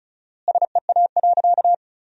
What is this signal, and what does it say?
Morse code, heard as a keyed tone, about 34 wpm, 700 Hz. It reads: SEA.
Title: 4x Word-Spacing - Sentences from Top 400 Words - 35wpm